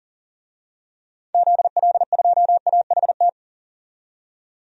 Morse code, heard as a keyed tone, about 40 wpm, 700 Hz. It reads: ZL2AHT
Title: DXCC-100 Call Signs - 6 Characters - 40wpm